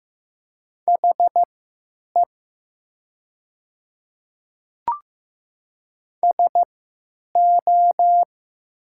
Morse code, heard as a keyed tone, about 15 wpm, spaced as for 9 wpm, 700 Hz. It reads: HE SO
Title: 3x Character Spacing - Top 100 Words - 15wpm